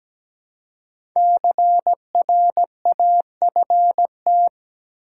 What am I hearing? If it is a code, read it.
Morse code, heard as a keyed tone, about 17 wpm, 700 Hz. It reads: CRAFT
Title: Five Letter Words - 17wpm